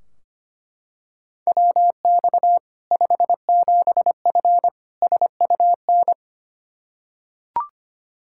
Morse code, heard as a keyed tone, about 25 wpm, 700 Hz. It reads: WX 57F SUN
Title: K1USN SST QSO Practice 25wpm